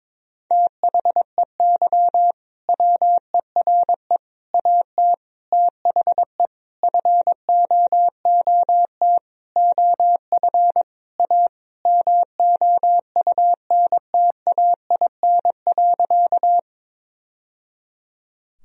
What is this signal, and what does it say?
Morse code, heard as a keyed tone, about 22 wpm, 700 Hz. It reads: THEY WERE AT THE FOOT OF A MOUNTAIN.
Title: Sentences from Top 1000 Words - Review Part 3 - 22wpm